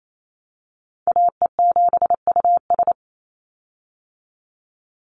Morse code, heard as a keyed tone, about 28 wpm, 700 Hz. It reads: AE7US
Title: Call Signs Repeated 3x - 28wpm